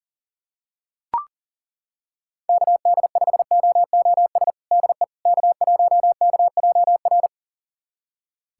Morse code, heard as a keyed tone, about 40 wpm, 700 Hz. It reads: KD5OOS DE K1KJR
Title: USA Callsign DE Callsign 40wpm